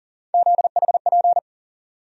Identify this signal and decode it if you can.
Morse code, heard as a keyed tone, about 40 wpm, 700 Hz. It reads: ZHP
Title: USA Call Sign Postfixes - 3 Letters 40wpm